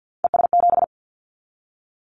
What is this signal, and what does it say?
Morse code, heard as a keyed tone, about 50 wpm, 700 Hz. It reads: ES7
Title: Callsign Prefixes for Top 100 DXCC - 50wpm